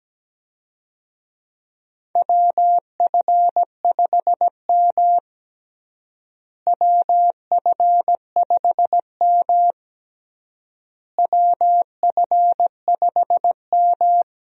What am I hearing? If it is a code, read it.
Morse code, heard as a keyed tone, about 17 wpm, 700 Hz. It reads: WF5M WF5M WF5M
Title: USA Call Signs - 4 Characters - Repeated 3x - 17wpm